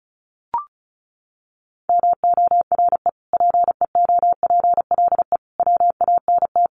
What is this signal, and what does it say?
Morse code, heard as a keyed tone, about 34 wpm, 700 Hz. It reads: MORE PEOPLE WANT
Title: Sentences from Top 1500 Words 35wpm